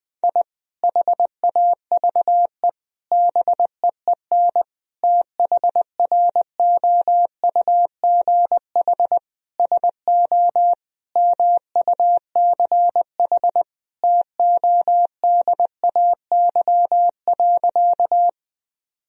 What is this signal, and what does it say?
Morse code, heard as a keyed tone, about 20 wpm, 700 Hz. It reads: I HAVE BEEN THROUGH SO MUCH TODAY.
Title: Sentences from Top 1000 Words - Review Part 4 - 20wpm